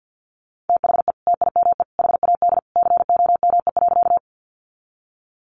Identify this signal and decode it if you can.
Morse code, heard as a keyed tone, about 50 wpm, 700 Hz. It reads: THE TIME HAD COME.
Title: Sentences from Top 100 Words 50wpm